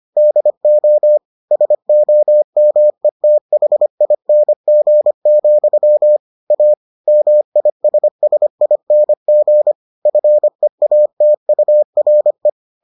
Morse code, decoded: DO SOMETHING, A MISSING FEATURE

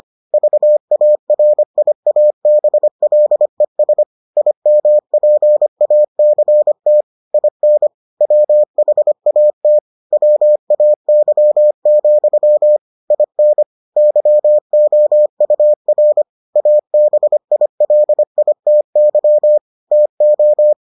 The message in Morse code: VARIABLES IMPACT IN WHAT WAY, IN YOUR ABILITY TO

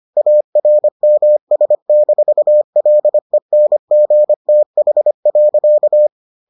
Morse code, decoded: ARMS-LENGTH.

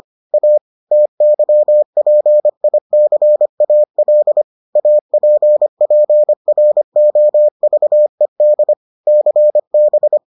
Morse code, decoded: A TYPICAL APPROVED CB

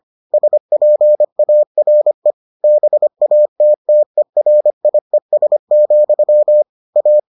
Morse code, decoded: SPARE BATTERIES, A

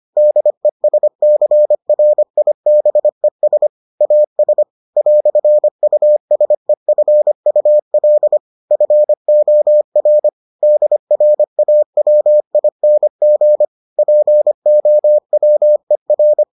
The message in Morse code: DESCRIBES AS "USEFUL FOR DRAWING POWER

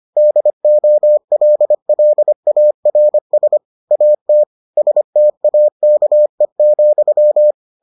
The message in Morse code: DOLLARS AT STAKE,

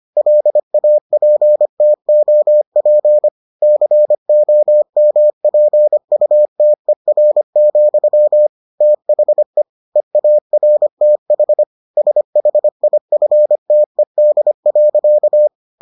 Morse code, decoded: LAPTOP COMPUTER, THE EARTH SHIFTED.

